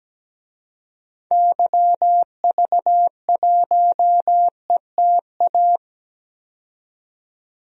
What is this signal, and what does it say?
Morse code, heard as a keyed tone, about 17 wpm, 700 Hz. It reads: YV1ETA
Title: DXCC-100 Call Signs - 6 Characters - 17wpm